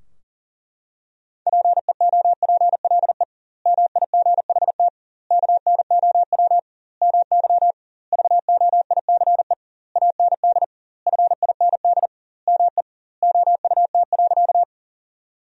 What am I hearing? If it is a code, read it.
Morse code, heard as a keyed tone, about 40 wpm, 700 Hz. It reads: PEOPLE MIGHT KNOW MY VOICE AND FIND ME OUT.
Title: 2x Word-Spacing - Sentences from Top 500 Words - 40wpm